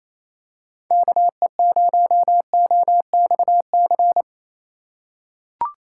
Morse code, heard as a keyed tone, about 28 wpm, 690 Hz. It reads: KE0OXC
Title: USA Call Signs - 3x Character Spacing - 28wpm